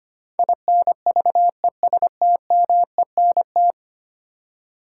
Morse code, heard as a keyed tone, about 25 wpm, 700 Hz. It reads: INVESTMENT
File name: 10 Letter Words - First 3 Letters Given - 25wpm